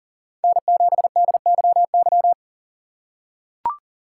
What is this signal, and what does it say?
Morse code, heard as a keyed tone, about 39 wpm, 700 Hz. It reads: N7DYY
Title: USA Call Signs - 5 Characters - 4x Character Spacing - 40wpm